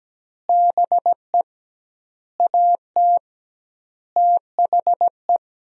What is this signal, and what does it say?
Morse code, heard as a keyed tone, about 17 wpm, 700 Hz. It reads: BE AT THE
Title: 2x Word-Spacing - Sentences from Top 400 Words - 17wpm